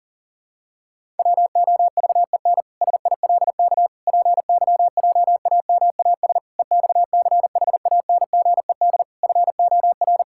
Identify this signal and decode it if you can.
Morse code, heard as a keyed tone, about 40 wpm, 700 Hz. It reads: WOVEN SILK PYJAMAS EXCHANGED FOR
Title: Pangram Challenge v2 - 40wpm